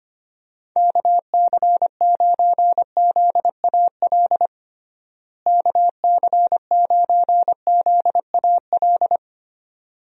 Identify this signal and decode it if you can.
Morse code, heard as a keyed tone, about 25 wpm, 700 Hz. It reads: KC9ZAL KC9ZAL
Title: USA Call Signs - 6 Characters - Repeated 2x - 25wpm